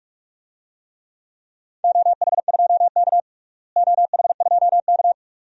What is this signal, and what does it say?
Morse code, heard as a keyed tone, about 46 wpm, 700 Hz. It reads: OH2K OH2K
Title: DXCC-100 Call Signs - 4 Characters - Repeated 2x - 45wpm